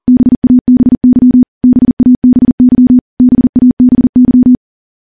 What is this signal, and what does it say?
Morse code, heard as a keyed tone, about 40 wpm, 250 Hz. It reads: BABY BABY BABY
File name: morse-training-40WPM